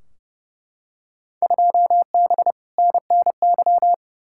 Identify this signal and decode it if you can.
Morse code, heard as a keyed tone, about 30 wpm, 700 Hz. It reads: 2B NNY
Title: Field Day Exchanges Encore - 30wpm